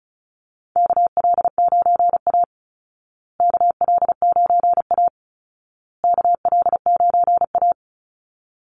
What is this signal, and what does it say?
Morse code, heard as a keyed tone, about 34 wpm, 700 Hz. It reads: KL9A KL9A KL9A
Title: Call Signs Repeated 2x - 35wpm